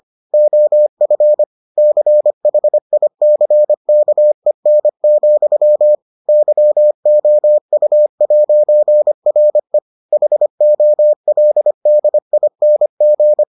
Morse code, heard as a keyed tone, about 25 wpm, 600 Hz. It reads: OF CHICKEN, YOU'RE HOLDING